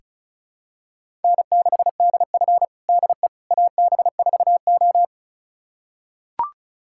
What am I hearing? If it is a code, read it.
Morse code, heard as a keyed tone, about 36 wpm, 700 Hz. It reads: N6DF DE AB4O
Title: USA Callsign DE Callsign - 4 Characters - 35wpm